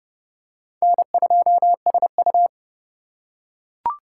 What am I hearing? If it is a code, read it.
Morse code, heard as a keyed tone, about 29 wpm, 700 Hz. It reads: N2SU